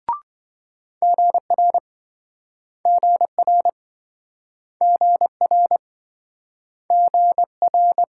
Morse code, heard as a keyed tone, about 26 wpm, 700 Hz. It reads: GR GR GR GR